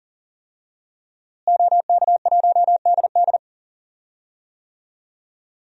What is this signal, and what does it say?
Morse code, heard as a keyed tone, about 40 wpm, 700 Hz. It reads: OK1DD
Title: DXCC-100 Call Signs - 5 Characters - Repeated 2x - 40wpm